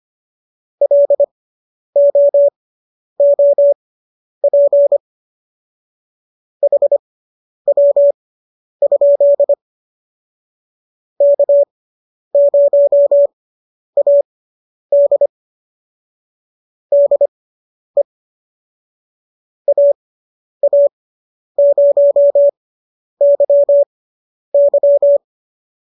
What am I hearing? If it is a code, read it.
Morse code, heard as a keyed tone, about 25 wpm, spaced as for 10 wpm, 570 Hz. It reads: LOOP HW? K0AD DE AA0YY